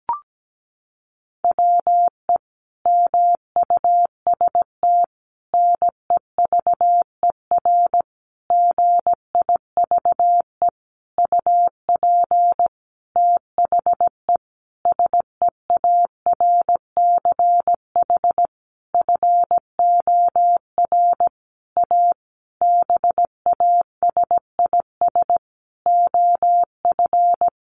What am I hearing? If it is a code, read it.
Morse code, heard as a keyed tone, about 17 wpm, 700 Hz. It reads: WE MUST NEVER GIVE UP THE SEARCH FOR A BASIS OF
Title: Sentences from Top 1500 Words 17wpm